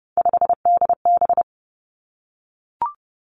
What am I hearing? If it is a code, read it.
Morse code, heard as a keyed tone, about 30 wpm, 700 Hz. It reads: 5DB
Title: Three Challenging Letter-Number Combinations 30wpm